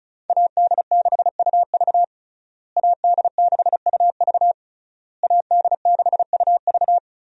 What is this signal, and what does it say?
Morse code, heard as a keyed tone, about 36 wpm, 700 Hz. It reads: AD6UV AD6UV AD6UV
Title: USA Call Signs - 5 Characters - Repeated 3x - 35wpm